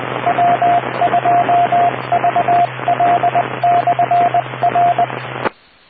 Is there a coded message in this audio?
Yes: Morse code from 0.3 to 5.0 s, heard as a keyed tone, about 20 wpm, 700 Hz, with radio noise. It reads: W2VL/R